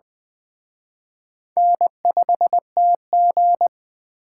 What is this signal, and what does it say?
Morse code, heard as a keyed tone, about 21 wpm, 700 Hz. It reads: N5TG
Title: Speed Racing - USA 1x2 Call Signs 17wpm